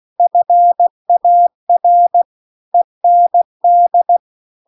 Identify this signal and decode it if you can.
Morse code, heard as a keyed tone, about 16 wpm, 700 Hz. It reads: FAR END
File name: CW-The-Big-Twoheaded-River-2-16WPM